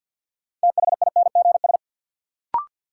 Morse code, heard as a keyed tone, about 47 wpm, 700 Hz. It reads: THINGS
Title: Sentences from Top 1400 Words - New Word Drill Alt - 50wpm